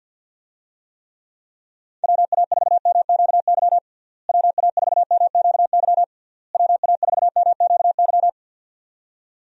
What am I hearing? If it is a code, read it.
Morse code, heard as a keyed tone, about 51 wpm, 700 Hz. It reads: WA4MQY WA4MQY WA4MQY
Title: USA Call Signs - 6 Characters - Repeated 3x - 50wpm